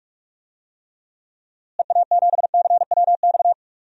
Morse code, heard as a keyed tone, about 43 wpm, 700 Hz. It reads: EA7CWX
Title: DXCC-100 Call Signs - 6 Characters - Repeated 2x - 45wpm